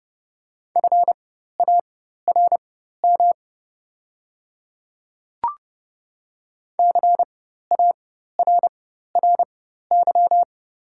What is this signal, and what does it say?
Morse code, heard as a keyed tone, about 30 wpm, spaced as for 14 wpm, 700 Hz. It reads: FARM CARRY
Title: 4x Character Spacing - Top 500 Words - 30wpm